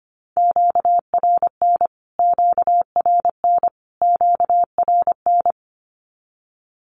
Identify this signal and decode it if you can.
Morse code, heard as a keyed tone, about 25 wpm, 700 Hz. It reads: QRN QRN QRN
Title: Q-Codes Repeated 3x 25wpm